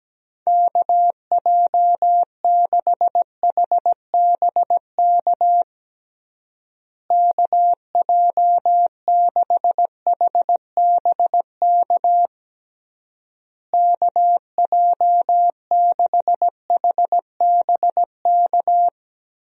Morse code, heard as a keyed tone, about 17 wpm, 700 Hz. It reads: KJ6HBK KJ6HBK KJ6HBK
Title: USA Call Signs - 6 Characters - Repeated 3x - 17wpm